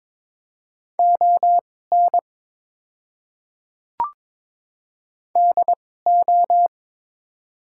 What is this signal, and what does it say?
Morse code, heard as a keyed tone, about 22 wpm, spaced as for 16 wpm, 700 Hz. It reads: ON DO
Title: 2x Character Spacing - Top 200 Words - 22wpm